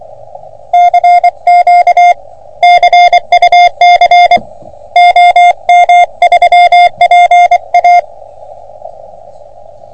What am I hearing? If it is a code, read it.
Morse code, heard as a keyed tone, about 24 wpm, 700 Hz. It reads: CQ CUC OM3PA
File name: OM3PA PILUP CUC 388